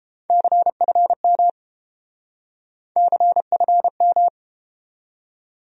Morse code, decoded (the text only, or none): CFM CFM